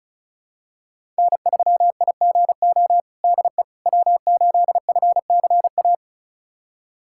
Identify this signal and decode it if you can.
Morse code, heard as a keyed tone, about 36 wpm, 700 Hz. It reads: N3IGO DE W8FCA
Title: USA Callsign DE Callsign - 5 Characters - 35wpm